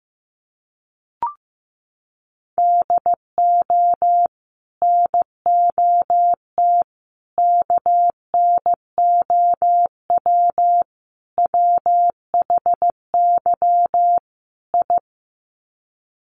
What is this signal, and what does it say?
Morse code, heard as a keyed tone, about 15 wpm, 700 Hz. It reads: DO NOT KNOW WHY I
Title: Sets of 5 Words Encore 15wpm